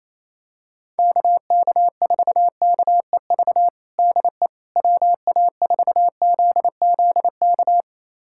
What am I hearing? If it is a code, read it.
Morse code, heard as a keyed tone, about 27 wpm, 700 Hz. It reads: KK4KEV DE WA4ZZK
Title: USA Callsign DE Callsign - 6 Characters - 28wpm